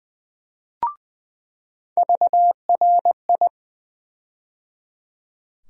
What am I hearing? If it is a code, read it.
Morse code, heard as a keyed tone, about 20 wpm, 700 Hz. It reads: VRI
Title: USA Call Sign Postfixes - 3 Letters 20wpm